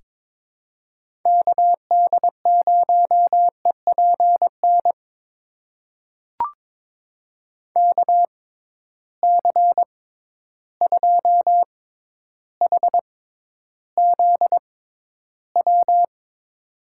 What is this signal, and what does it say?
Morse code, heard as a keyed tone, about 21 wpm, spaced as for 11 wpm, 700 Hz. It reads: KD0EPN KC2HZW
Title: USA Call Signs - 6 Characters - 6x Character Spacing - 22wpm